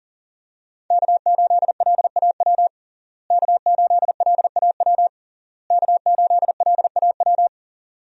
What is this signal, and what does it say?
Morse code, heard as a keyed tone, about 40 wpm, 700 Hz. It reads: K8LAW K8LAW K8LAW
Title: USA Call Signs - 5 Characters - Repeated 3x - 40wpm